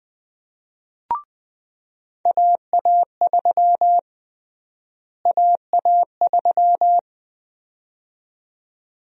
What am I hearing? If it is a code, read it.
Morse code, heard as a keyed tone, about 20 wpm, 700 Hz. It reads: AA3 AA3
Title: USA Call Sign Prefixes - Repeated 2x - 20wpm